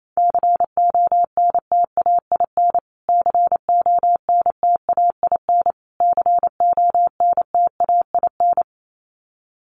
Morse code, heard as a keyed tone, about 28 wpm, 700 Hz. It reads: CONTAIN CONTAIN CONTAIN
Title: Top 500 Words Repeated 28wpm